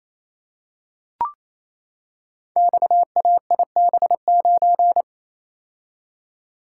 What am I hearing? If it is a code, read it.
Morse code, heard as a keyed tone, about 27 wpm, 700 Hz. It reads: XAIB9
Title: Code Groups - 5 Letter:Numbers - 28wpm